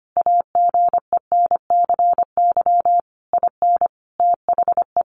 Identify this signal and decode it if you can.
Morse code, heard as a keyed tone, about 25 wpm, 700 Hz. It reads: AGENCY IN THE
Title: Sentences from Top 1300 Words 25wpm